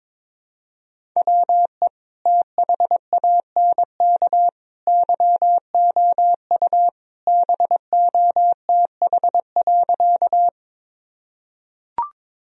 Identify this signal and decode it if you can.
Morse code, heard as a keyed tone, about 22 wpm, 700 Hz. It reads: WE THANK YOU BOTH.
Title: Sentences from Top 1000 Words - Review Part 1 - 22wpm